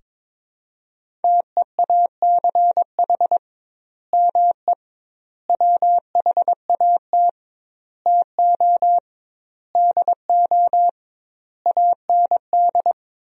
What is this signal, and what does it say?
Morse code, heard as a keyed tone, about 22 wpm, 700 Hz. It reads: TEACH ME WHAT TO DO AND
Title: 2x Word-Spacing - Sentences from Top 500 Words - 22wpm